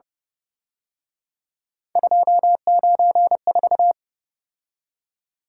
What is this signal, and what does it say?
Morse code, heard as a keyed tone, about 30 wpm, 700 Hz. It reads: 294